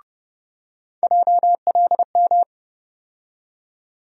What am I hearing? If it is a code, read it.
Morse code, heard as a keyed tone, about 30 wpm, 700 Hz. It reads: JLM
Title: USA Call Sign Postfixes - 3 Letters 30wpm